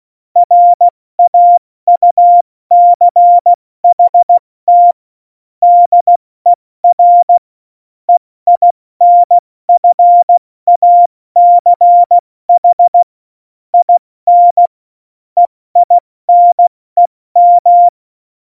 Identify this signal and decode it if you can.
Morse code, heard as a keyed tone, about 16 wpm, 700 Hz. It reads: RAUCHT DER EINFACH IN EINEM